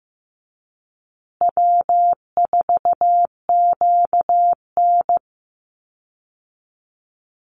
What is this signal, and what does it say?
Morse code, heard as a keyed tone, about 15 wpm, 700 Hz. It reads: W4QN
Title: Easy Call Signs Repeated 3x - 15wpm